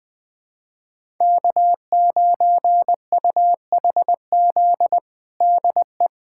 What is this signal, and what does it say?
Morse code, heard as a keyed tone, about 21 wpm, 700 Hz. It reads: K9UHZ DE
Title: USA Callsign DE Callsign - 5 Characters - 20wpm